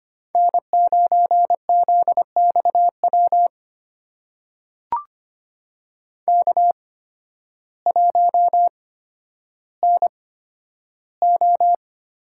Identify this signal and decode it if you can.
Morse code, heard as a keyed tone, about 25 wpm, spaced as for 12 wpm, 700 Hz. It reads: N9ZXW K1NO
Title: USA Call Signs - 5 Characters - 8x Character Spacing - 25wpm